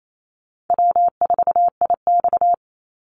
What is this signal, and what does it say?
Morse code, heard as a keyed tone, about 28 wpm, 700 Hz. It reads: W4IX
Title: Easy Call Signs - 28wpm